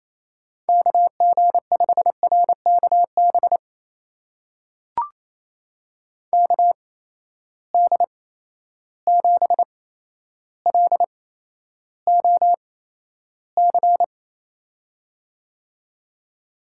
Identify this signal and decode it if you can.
Morse code, heard as a keyed tone, about 28 wpm, spaced as for 12 wpm, 700 Hz. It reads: KG5RKB KD7LOC
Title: USA Call Signs - 6 Characters - 8x Character Spacing - 28wpm